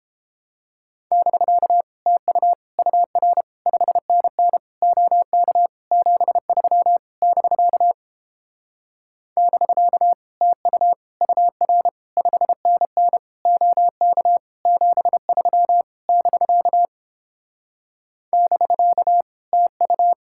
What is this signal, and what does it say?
Morse code, heard as a keyed tone, about 29 wpm, 700 Hz. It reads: <BK> TU UR 5NN OK 73 <BK> <BK> TU UR 5NN OK 73 <BK> <BK> TU